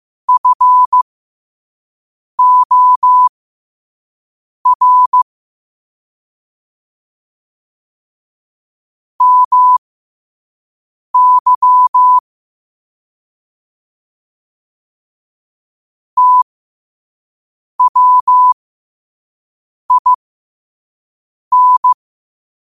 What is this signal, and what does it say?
Morse code, heard as a keyed tone, about 15 wpm, spaced as for 5 wpm, 1000 Hz. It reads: FOR MY TWIN
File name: qotdfiles / 5wpm